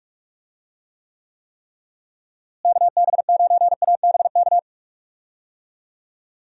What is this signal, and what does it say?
Morse code, heard as a keyed tone, about 41 wpm, 700 Hz. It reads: KB9ABK
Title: USA Call Signs - 6 Characters - Repeated 3x - 45wpm